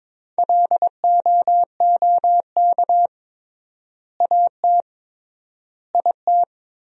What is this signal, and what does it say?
Morse code, heard as a keyed tone, about 22 wpm, 700 Hz. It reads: LOOK AT IT
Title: 3x Word-Spacing - Sentences from Top 200 Words - 22wpm